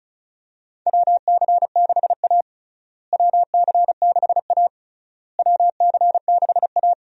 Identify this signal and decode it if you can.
Morse code, heard as a keyed tone, about 34 wpm, 700 Hz. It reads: WC6A WC6A WC6A
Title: USA Call Signs - 4 Characters - Repeated 3x - 35wpm